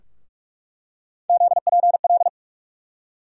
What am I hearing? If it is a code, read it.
Morse code, heard as a keyed tone, about 46 wpm, 700 Hz. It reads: ZPL
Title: USA Call Sign Postfixes - 3 Letters 45wpm